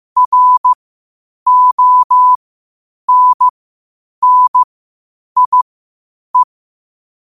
Morse code, heard as a keyed tone, about 15 wpm, spaced as for 8 wpm, 1000 Hz. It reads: RONNIE
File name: 8wpm